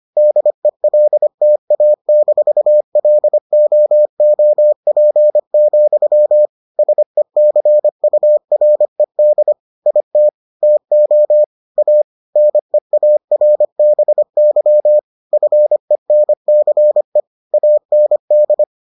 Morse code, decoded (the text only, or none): DELTA-LOOP, SECURED IT TO A NEARBY FENCE AND